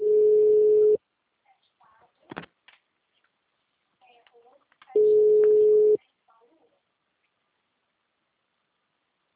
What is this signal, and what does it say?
Ringback tone (Europe, 425 Hz), with speech.